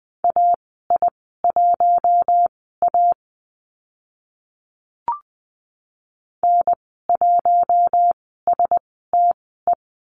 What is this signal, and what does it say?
Morse code, heard as a keyed tone, about 20 wpm, spaced as for 14 wpm, 700 Hz. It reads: AI1A N1STE
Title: 042 Lesson 8g - Number 1 - Call Signs 20wpm